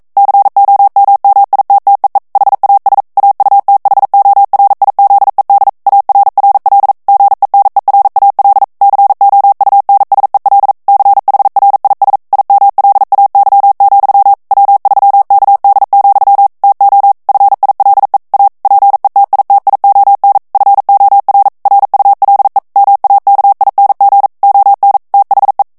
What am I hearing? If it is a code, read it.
Morse code, heard as a keyed tone, about 42 wpm, 800 Hz. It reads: COMMITTEE HAS AUTHORIZED ARRL GENERAL COUNSEL CHRIS IMLAY, W3KD, TO FILE A PETITION FOR RULE MAKING ON THE
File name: 40 mots minute